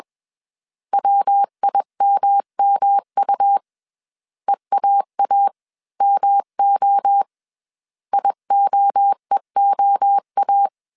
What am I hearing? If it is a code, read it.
Morse code, heard as a keyed tone, about 21 wpm, 800 Hz. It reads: WIMMU EAAMO IOEOA
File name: CW / DET / 1 3 1500C S B 20wpm 1754